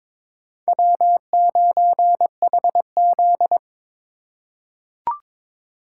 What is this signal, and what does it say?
Morse code, heard as a keyed tone, about 22 wpm, 700 Hz. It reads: W9HZ
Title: USA Call Signs - 4 Characters - 3x Character Spacing - 22wpm